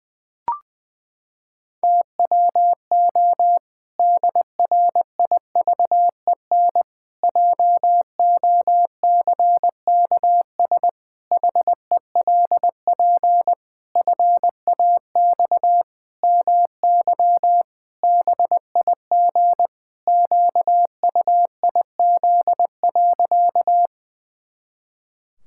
Morse code, heard as a keyed tone, about 20 wpm, 700 Hz. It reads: TWO DRIVEN JOCKS HELP FAX MY BIG QUIZ.
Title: Pangram Challenge v2 - 20wpm